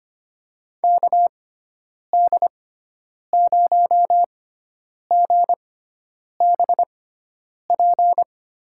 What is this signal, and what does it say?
Morse code, heard as a keyed tone, about 25 wpm, spaced as for 9 wpm, 700 Hz. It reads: KD0GBP
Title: USA Call Signs - 6x Character Spacing - 25wpm